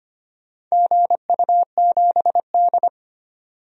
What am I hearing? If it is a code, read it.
Morse code, heard as a keyed tone, about 25 wpm, 700 Hz. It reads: GU7D
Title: DXCC-100 Call Signs - 4 Characters - 25wpm